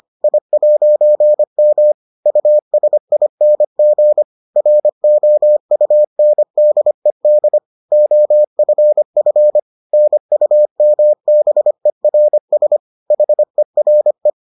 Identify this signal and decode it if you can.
Morse code, heard as a keyed tone, about 25 wpm, 600 Hz. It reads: I'M USING ROUNDED OFF NUMBERS HERE